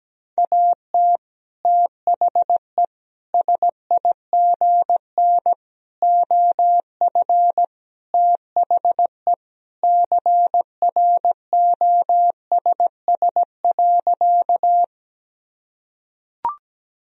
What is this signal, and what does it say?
Morse code, heard as a keyed tone, about 17 wpm, 700 Hz. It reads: AT THE SIGN OF THE CROSS.
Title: Sentences from Top 1000 Words - Review Part 1 - 17wpm